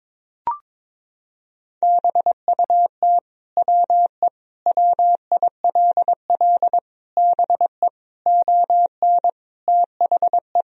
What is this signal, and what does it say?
Morse code, heard as a keyed tone, about 22 wpm, 700 Hz. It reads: BUT WE WILL BE ON THE
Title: Sentences from Top 1000 Words - Review Part 3 - 22wpm